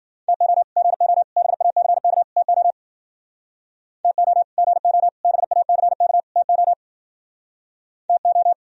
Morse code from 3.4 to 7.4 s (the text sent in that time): TO GO BACK TO